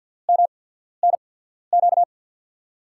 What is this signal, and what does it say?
Morse code, heard as a keyed tone, about 53 wpm, spaced as for 14 wpm, 700 Hz. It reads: MNQ
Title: USA Call Signs - 5 Characters - 8x Character Spacing - 50wpm